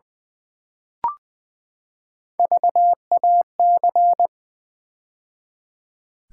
Morse code, heard as a keyed tone, about 20 wpm, 700 Hz. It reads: VAC
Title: Build A Word - 3 Letters At A Time - 6-Letter Words - 20wpm